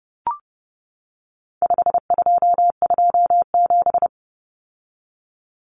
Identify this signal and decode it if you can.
Morse code, heard as a keyed tone, about 30 wpm, 700 Hz. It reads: 5227
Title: Common Cut Numbers - 4-Digit 30wpm